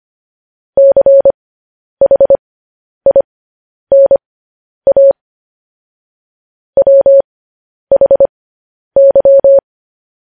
Morse code, heard as a keyed tone, about 25 wpm, spaced as for 10 wpm, 550 Hz. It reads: CHINA WHY
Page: news headlines in morse code at 10 wpm